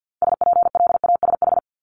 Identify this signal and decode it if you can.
Morse code, heard as a keyed tone, about 50 wpm, 700 Hz. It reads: SPLASH
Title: Multi-letter Phonemes 50wpm